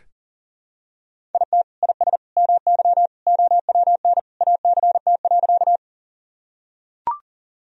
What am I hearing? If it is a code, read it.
Morse code, heard as a keyed tone, about 40 wpm, 700 Hz. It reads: IT IS MY OWN ACT.
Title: Sentences from Top 1000 Words - Review Part 1 - 40wpm